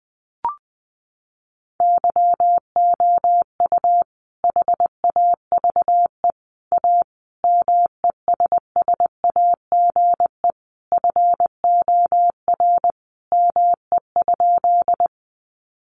Morse code, heard as a keyed tone, about 20 wpm, 700 Hz. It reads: YOU HAVE A MESSAGE FOR ME?
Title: Sentences from Top 1100 Words 20wpm